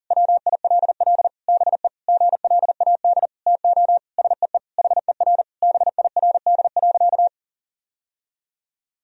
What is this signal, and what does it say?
Morse code, heard as a keyed tone, about 40 wpm, 700 Hz. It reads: WILL BE GLAD TO SEE HER BIRD.
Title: Sentences from Top 1000 Words - Review Part 2 - 40wpm